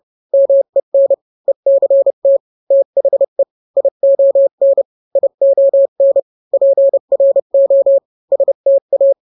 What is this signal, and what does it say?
Morse code, heard as a keyed tone, about 30 wpm, 550 Hz. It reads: MEN ECT THE ION ION PRO STA